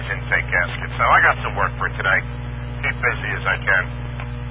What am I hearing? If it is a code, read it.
Emergency Alert System (SAME) tones, heard as harsh buzzing bursts, with speech. It is heard from 0.2 to 3.9 s.